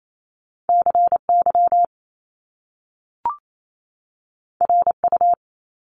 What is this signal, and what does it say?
Morse code, heard as a keyed tone, about 28 wpm, 700 Hz. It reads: CY RU
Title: Top Two Letter Combinations in Words 28wpm